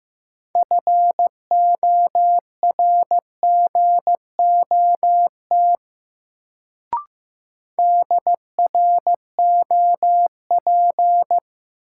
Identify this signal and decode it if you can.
Morse code, heard as a keyed tone, about 15 wpm, 700 Hz. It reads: FORGOT DROP
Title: Sentences from Top 1400 Words - New Word Drill - 15wpm